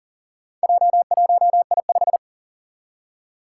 Morse code, heard as a keyed tone, about 37 wpm, 700 Hz. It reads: J1I5
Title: Four Challenging Letter-Number Combinations 40wpm